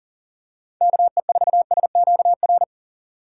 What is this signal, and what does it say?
Morse code, heard as a keyed tone, about 40 wpm, 700 Hz. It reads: KE4SQR
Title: USA Call Signs - 6 Characters - 2x Character Spacing - 40wpm